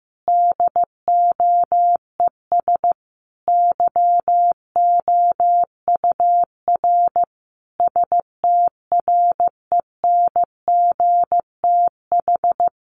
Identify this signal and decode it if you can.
Morse code, heard as a keyed tone, about 14 wpm, 700 Hz. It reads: DOES YOUR STRENGTH
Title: Sentences from Top 1400 Words 15wpm